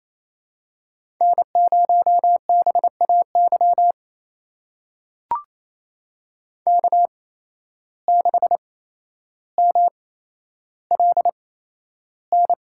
Morse code, heard as a keyed tone, about 28 wpm, spaced as for 12 wpm, 700 Hz. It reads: N0BAY K6MLN
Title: USA Call Signs - 5 Characters - 8x Character Spacing - 28wpm